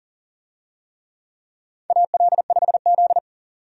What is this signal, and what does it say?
Morse code, heard as a keyed tone, about 40 wpm, 700 Hz. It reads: AL5Z
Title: USA Call Signs - 4 Characters - 40wpm